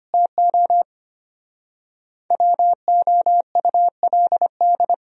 Morse code, heard as a keyed tone, about 31 wpm, 700 Hz. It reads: TO WOULD
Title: Speed-Racing - Sets of 3 Words 25wpm